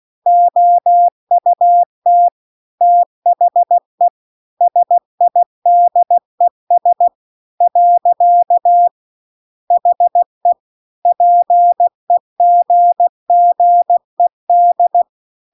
Morse code, heard as a keyed tone, about 16 wpm, 700 Hz. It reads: OUT THE SIDES . HE PEGGED